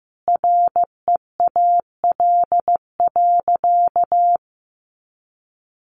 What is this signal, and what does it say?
Morse code, heard as a keyed tone, about 15 wpm, 700 Hz. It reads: REAL.
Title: Sentences from Top 300 Words 15wpm